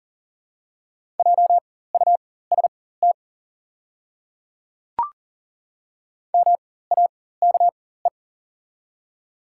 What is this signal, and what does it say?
Morse code, heard as a keyed tone, about 40 wpm, spaced as for 19 wpm, 690 Hz. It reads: JUST MAKE
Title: 4x Character Spacing - Top 300 Words - 40wpm